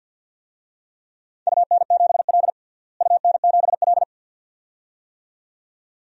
Morse code, heard as a keyed tone, about 53 wpm, 700 Hz. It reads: UN7L UN7L
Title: DXCC-100 Call Signs - 4 Characters - Repeated 2x - 50wpm